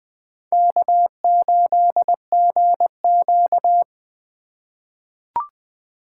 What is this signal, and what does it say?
Morse code, heard as a keyed tone, about 20 wpm, 700 Hz. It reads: K8GQ
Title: 2023 POTA Hunter Call Signs - 20wpm